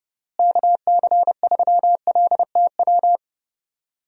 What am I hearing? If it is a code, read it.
Morse code, heard as a keyed tone, about 30 wpm, 700 Hz. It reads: KC3LTW
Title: USA Call Signs - 6 Characters - Repeated 3x - 30wpm